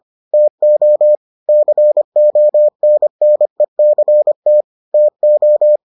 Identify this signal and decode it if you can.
Morse code, heard as a keyed tone, about 25 wpm, 600 Hz. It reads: TO CONNECT TO